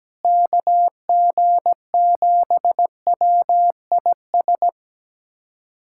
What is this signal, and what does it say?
Morse code, heard as a keyed tone, about 17 wpm, 700 Hz. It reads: KG7WIS
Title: USA Callsign DE Callsign 17wpm